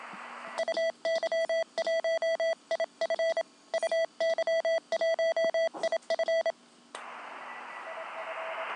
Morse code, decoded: UY1 I F UY1IF IF